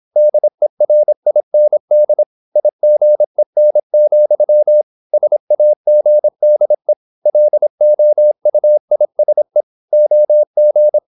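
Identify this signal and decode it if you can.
Morse code, heard as a keyed tone, about 26 wpm, 600 Hz. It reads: DERIND IGEN, SAGDE LOUISE OG
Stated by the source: High-speed (QRQ) Morse at 26 wpm.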